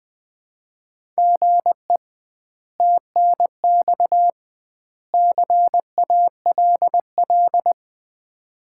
Morse code, heard as a keyed tone, about 20 wpm, 700 Hz. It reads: GE TNX CALL
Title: Common QSO Phrases - 2x Word-Spacing - 20wpm